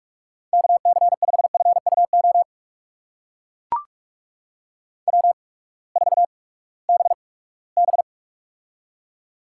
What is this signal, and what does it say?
Morse code, heard as a keyed tone, about 42 wpm, spaced as for 22 wpm, 700 Hz. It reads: KC5FUO W4BB
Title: USA Call Signs - 8x Character Spacing - 45wpm